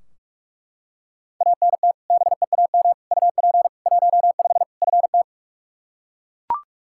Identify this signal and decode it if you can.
Morse code, heard as a keyed tone, about 42 wpm, 700 Hz. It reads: ANT BEAM UP 15 FT
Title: Practice QSO - Exchange 2 - Info - 45wpm